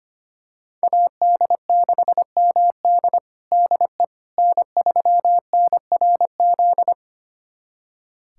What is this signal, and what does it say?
Morse code, heard as a keyed tone, about 25 wpm, 700 Hz. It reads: AD6MD DE N3NRZ
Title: USA Callsign DE Callsign - 5 Characters - 25wpm